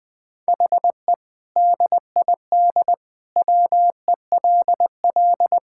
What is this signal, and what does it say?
Morse code, heard as a keyed tone, about 20 wpm, 700 Hz. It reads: HE DID WELL
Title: Sentences with 2-Word Phrases - Spoken Beginning - 20wpm